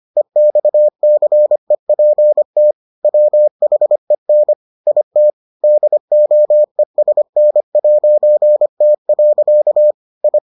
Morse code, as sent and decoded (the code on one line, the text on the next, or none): . -..- -.-. .--. - / .-- .... -. / .. - / -.. --- . ... -. .----. - .-.-.- / ..
EXCEPT WHEN IT DOESN'T. I